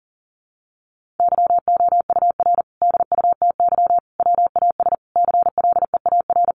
Morse code, heard as a keyed tone, about 37 wpm, 700 Hz. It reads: YOUR DUTY WAS CLEAR
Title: Sentences from Top 1500 Words 40wpm